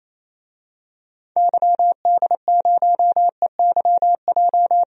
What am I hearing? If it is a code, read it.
Morse code, heard as a keyed tone, about 28 wpm, 700 Hz. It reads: YD0EYJ
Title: DXCC-100 Call Signs - 6 Characters - Repeated 2x - 28wpm